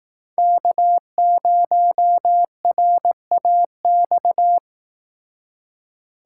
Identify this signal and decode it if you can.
Morse code, heard as a keyed tone, about 18 wpm, 700 Hz. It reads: K0RAX
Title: Speed Racing - USA 1x3 Call Signs 15wpm